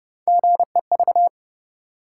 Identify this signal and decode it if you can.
Morse code, heard as a keyed tone, about 28 wpm, 700 Hz. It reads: GEV